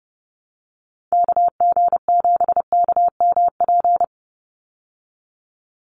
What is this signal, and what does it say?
Morse code, heard as a keyed tone, about 30 wpm, 700 Hz. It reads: KG7KMP
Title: USA Call Signs 30wpm